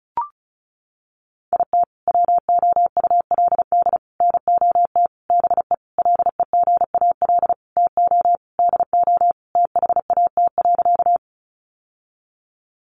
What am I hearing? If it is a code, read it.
Morse code, heard as a keyed tone, about 34 wpm, 700 Hz. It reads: IT WOULD NOT BE LEGAL TO DO THAT.
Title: Sentences from Top 1200 Words 35wpm